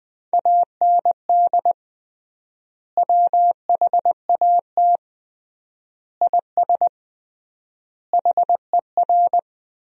Morse code, heard as a keyed tone, about 21 wpm, 700 Hz. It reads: AND WHAT IS HER
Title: 3x Word-Spacing - Sentences from Top 200 Words - 20wpm